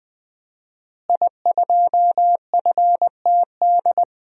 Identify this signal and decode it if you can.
Morse code, heard as a keyed tone, about 21 wpm, 700 Hz. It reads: I2FTD
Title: DXCC-100 Call Signs - 5 Characters - Repeated 3x - 20wpm